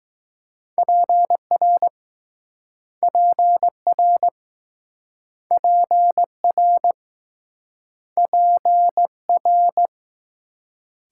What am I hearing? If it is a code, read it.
Morse code, heard as a keyed tone, about 20 wpm, 700 Hz. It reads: PR PR PR PR